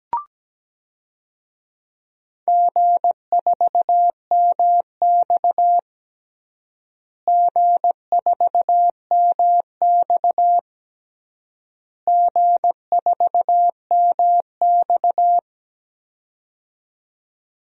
Morse code, heard as a keyed tone, about 17 wpm, 700 Hz. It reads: G4MX G4MX G4MX
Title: DXCC-100 Call Signs - 4 Characters - Repeated 3x - 17wpm